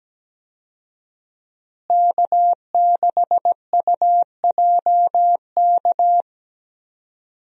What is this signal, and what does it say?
Morse code, heard as a keyed tone, about 17 wpm, 700 Hz. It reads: K6UJK
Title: USA Call Signs - 5 Characters - Repeated 2x - 17wpm